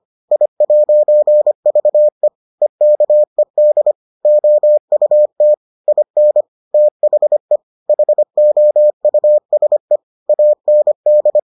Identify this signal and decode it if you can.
Morse code, heard as a keyed tone, about 25 wpm, 600 Hz. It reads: I'VE EKED OUT IN THE HOUSE AND